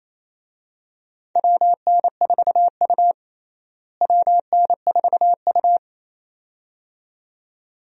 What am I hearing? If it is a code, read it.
Morse code, heard as a keyed tone, about 28 wpm, 700 Hz. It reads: WN4U WN4U
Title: USA Call Signs - 4 Characters - Repeated 2x - 28wpm